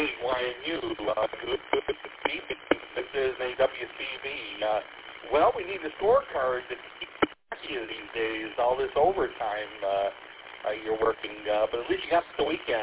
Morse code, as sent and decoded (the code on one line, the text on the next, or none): none